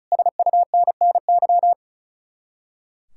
Morse code, heard as a keyed tone, about 44 wpm, 700 Hz. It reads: SUNNY